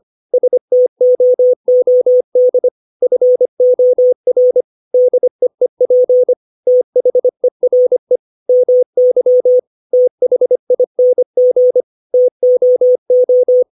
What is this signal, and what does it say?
Morse code, heard as a keyed tone, about 25 wpm, 500 Hz. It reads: STOOD FOR DEEP THERE MY THING TOO
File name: 632 words / 632 Words 25WPM